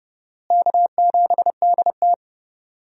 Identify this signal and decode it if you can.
Morse code, heard as a keyed tone, about 32 wpm, 700 Hz. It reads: K7DT
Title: USA Call Signs - 4 Characters - Repeated 2x - 30wpm